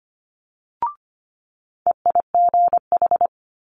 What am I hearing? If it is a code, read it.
Morse code, heard as a keyed tone, about 25 wpm, 700 Hz. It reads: EIGH
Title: Multi-letter Phonemes Quick Fire 25wpm